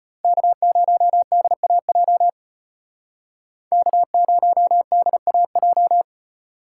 Morse code, heard as a keyed tone, about 38 wpm, 700 Hz. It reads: K0DAJ K0DAJ